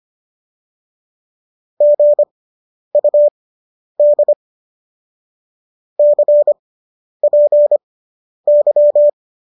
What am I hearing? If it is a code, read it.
Morse code, heard as a keyed tone, about 25 wpm, spaced as for 10 wpm, 590 Hz. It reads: GUD CPY